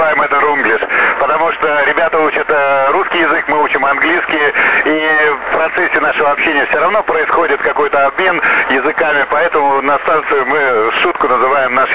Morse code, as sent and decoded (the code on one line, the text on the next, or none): none